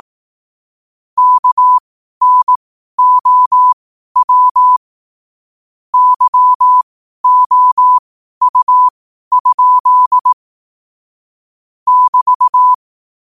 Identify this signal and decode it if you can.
Morse code, heard as a keyed tone, about 18 wpm, spaced as for 12 wpm, 1000 Hz. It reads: KNOW YOU? <BT>
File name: qotdfiles / 12wpm